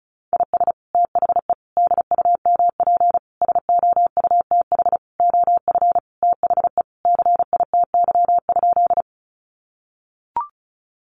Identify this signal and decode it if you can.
Morse code, heard as a keyed tone, about 34 wpm, 700 Hz. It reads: IS THE DUMP SOUTH OF THE CITY?
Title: Sentences from Top 1300 Words 35wpm